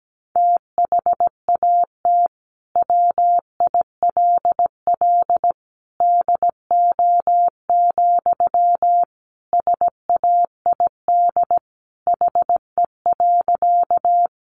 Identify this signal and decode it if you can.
Morse code, heard as a keyed tone, about 17 wpm, 700 Hz. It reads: THAT WILL DO, SAID HE.
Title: Sentences from Top 100 Words 17wpm